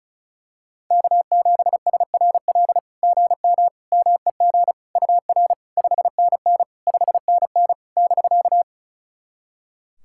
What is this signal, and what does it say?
Morse code, heard as a keyed tone, about 36 wpm, 700 Hz. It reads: K7SRL GM MEG UR 5NN 5NN <BK>
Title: SOTA QSO Practice 35wpm